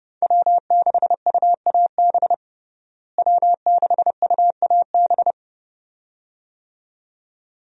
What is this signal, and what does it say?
Morse code, heard as a keyed tone, about 30 wpm, 700 Hz. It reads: W6UAB W6UAB
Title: USA Call Signs - 5 Characters - Repeated 2x - 30wpm